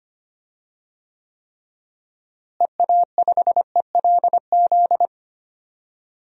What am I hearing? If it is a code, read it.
Morse code, heard as a keyed tone, about 25 wpm, 700 Hz. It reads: EA5ELZ
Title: DXCC-100 Call Signs - 6 Characters - Repeated 3x - 25wpm